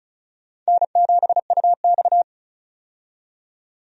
Morse code, heard as a keyed tone, about 34 wpm, 700 Hz. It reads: N7UX
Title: USA Call Signs - 4 Characters - 35wpm